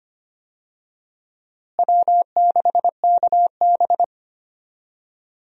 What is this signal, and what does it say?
Morse code, heard as a keyed tone, about 25 wpm, 700 Hz. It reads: W6KB
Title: USA Call Signs - 4 Characters - Repeated 3x - 25wpm